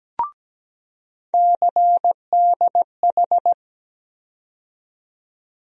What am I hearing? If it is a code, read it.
Morse code, heard as a keyed tone, about 17 wpm, 700 Hz. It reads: CDH
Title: USA Call Sign Postfixes - 3 Letters 17wpm